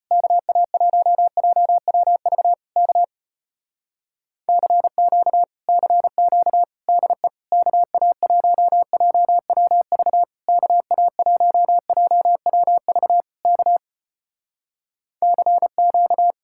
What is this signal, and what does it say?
Morse code, heard as a keyed tone, about 34 wpm, 700 Hz. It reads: KA1JWV K CQ CQ DE KA1JWV KA1JWV K CQ